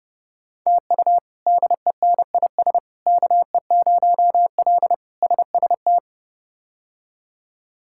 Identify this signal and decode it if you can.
Morse code, heard as a keyed tone, about 32 wpm, 700 Hz. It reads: TU DENIS KE0L SST
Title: K1USN SST QSO Practice - 30wpm